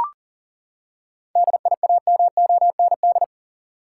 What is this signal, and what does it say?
Morse code, heard as a keyed tone, about 40 wpm, 700 Hz. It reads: DIAMOND
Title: Seven Letter Words - 40wpm